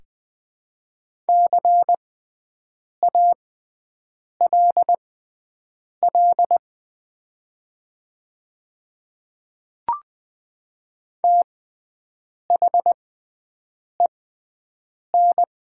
Morse code, heard as a keyed tone, about 20 wpm, spaced as for 7 wpm, 700 Hz. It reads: CALL THEN
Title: 6x Character Spacing - Top 100 Words - 20wpm